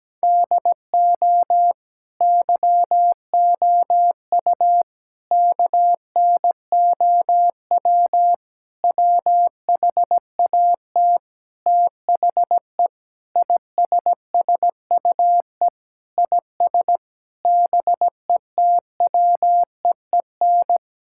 Morse code, heard as a keyed tone, about 17 wpm, 700 Hz. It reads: DO YOU KNOW WHAT THE ISSUE IS BETWEEN
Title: Sentences from Top 1000 Words - Review Part 6 - 17wpm